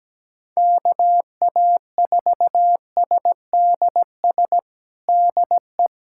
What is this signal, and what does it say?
Morse code, heard as a keyed tone, about 17 wpm, 700 Hz. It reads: KA4SDS DE
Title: USA Callsign DE Callsign 17wpm